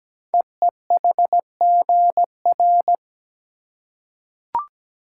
Morse code, heard as a keyed tone, about 17 wpm, 700 Hz. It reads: EEHGR
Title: Code Groups - 5 Letters - 17wpm